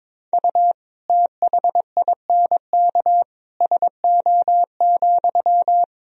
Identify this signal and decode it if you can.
Morse code, heard as a keyed tone, about 21 wpm, 700 Hz. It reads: U THINK SO,
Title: Sentences from Top 200 Words and Abbreviations 22wpm